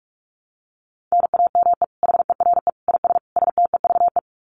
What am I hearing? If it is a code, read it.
Morse code, heard as a keyed tone, about 44 wpm, 700 Hz. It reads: NAME HERE IS STEVE
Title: Common QSO Phrases 45wpm